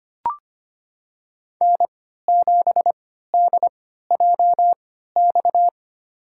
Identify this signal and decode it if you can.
Morse code, heard as a keyed tone, about 25 wpm, spaced as for 14 wpm, 700 Hz. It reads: N7DJX
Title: USA Call Signs - 5 Characters - 3x Character Spacing - 25wpm